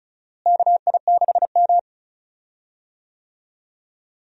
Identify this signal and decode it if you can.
Morse code, heard as a keyed tone, about 36 wpm, 700 Hz. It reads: KI6M
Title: USA Call Signs - 4 Characters - Repeated 2x - 35wpm